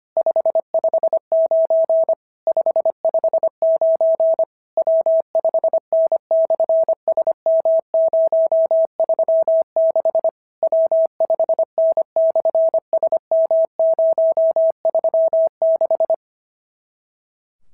Morse code, heard as a keyed tone, about 25 wpm, 650 Hz. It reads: 559 559 W5N/SM036 W5N/SM036
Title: SOTA QSO Practice 25wpm